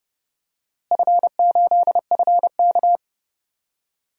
Morse code, heard as a keyed tone, about 30 wpm, 700 Hz. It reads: F8FK
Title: DXCC-100 Call Signs - 4 Characters - Repeated 2x - 30wpm